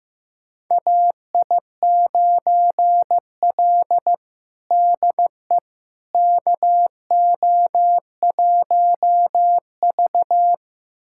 Morse code, heard as a keyed tone, about 16 wpm, 700 Hz. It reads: AI9L DE KO1V
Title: USA Callsign DE Callsign - 4 Characters - 15wpm